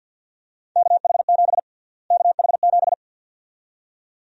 Morse code, heard as a keyed tone, about 51 wpm, 700 Hz. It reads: KH7 KH7
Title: USA Call Sign Prefixes - Repeated 2x - 50wpm